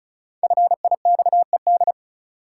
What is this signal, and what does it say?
Morse code, heard as a keyed tone, about 34 wpm, 700 Hz. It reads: FIXED
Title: Five Letter Words - 35wpm